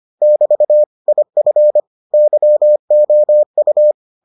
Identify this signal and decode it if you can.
Morse code, heard as a keyed tone, about 25 wpm, 600 Hz. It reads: <BT> IF YOU